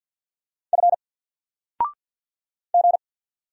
Morse code, heard as a keyed tone, about 50 wpm, 700 Hz. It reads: F G
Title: Single Letter-Number 50wpm